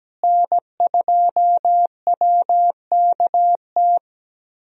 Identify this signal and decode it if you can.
Morse code, heard as a keyed tone, about 17 wpm, 700 Hz. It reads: N2WKT
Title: USA Call Signs - 5 Characters - Repeated 3x - 17wpm